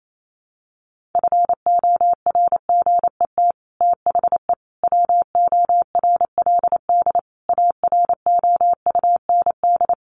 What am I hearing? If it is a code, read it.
Morse code, heard as a keyed tone, about 28 wpm, 700 Hz. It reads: FORGET THE WORLD AROUND
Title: Sentences from Top 1200 Words 28wpm